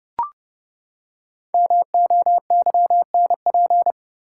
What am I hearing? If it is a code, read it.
Morse code, heard as a keyed tone, about 32 wpm, 700 Hz. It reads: MOYNP
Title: Code Groups - 5 Letter:Numbers - 30wpm